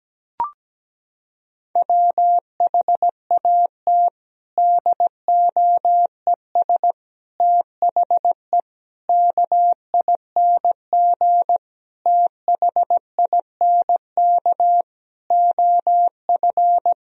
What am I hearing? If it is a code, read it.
Morse code, heard as a keyed tone, about 17 wpm, 700 Hz. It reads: WHAT DOES THE KING THINK OF
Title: Sentences from Top 1000 Words - Review Part 2 - 17wpm